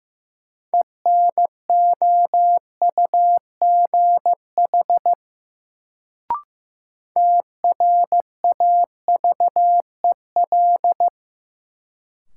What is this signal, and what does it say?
Morse code, heard as a keyed tone, about 15 wpm, 700 Hz. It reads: ENOUGH TRAVEL
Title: Sentences from Top 400 Words - New Word Drill - 15wpm